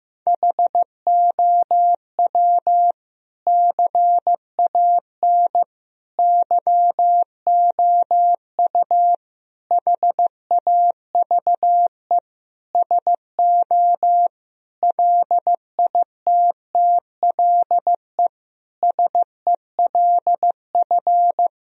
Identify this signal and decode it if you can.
Morse code, heard as a keyed tone, about 15 wpm, 700 Hz. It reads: HOW CAN YOU HAVE SO LITTLE SELF
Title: Sentences from Top 1000 Words - Review Part 1 - 15wpm